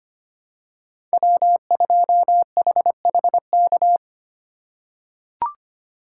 Morse code, heard as a keyed tone, about 25 wpm, 700 Hz. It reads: W2HHK
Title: USA Call Signs - 5 Characters - 4x Character Spacing - 25wpm